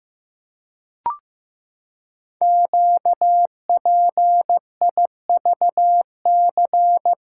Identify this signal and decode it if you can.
Morse code, heard as a keyed tone, about 15 wpm, 700 Hz. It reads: QPIVC
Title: Code Groups - 5 Letters - 15wpm